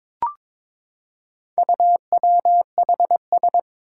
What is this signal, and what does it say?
Morse code, heard as a keyed tone, about 22 wpm, 700 Hz. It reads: UWHS
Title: Four Challenging Letter-Number Combinations 22wpm